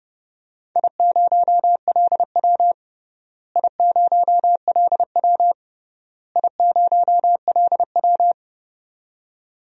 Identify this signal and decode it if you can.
Morse code, heard as a keyed tone, about 30 wpm, 700 Hz. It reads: I0LW I0LW I0LW
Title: DXCC-100 Call Signs - 4 Characters - Repeated 3x - 30wpm